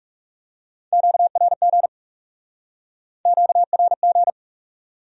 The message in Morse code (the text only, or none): QRG QRG